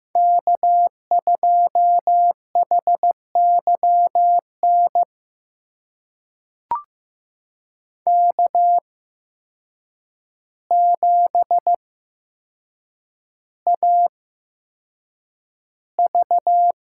Morse code, heard as a keyed tone, about 15 wpm, spaced as for 7 wpm, 700 Hz. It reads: K2HYN K7AV
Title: USA Call Signs - 5 Characters - 8x Character Spacing - 15wpm